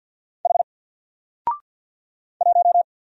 Morse code, heard as a keyed tone, about 48 wpm, 700 Hz. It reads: H 1
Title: Single Letter-Number Alt 50wpm